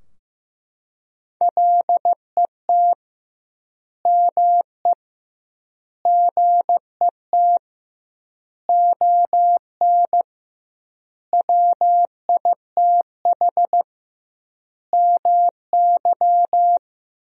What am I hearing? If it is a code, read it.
Morse code, heard as a keyed tone, about 15 wpm, 700 Hz. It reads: LET ME GET ON WITH MY
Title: 2x Word-Spacing - Sentences from Top 300 Words - 15wpm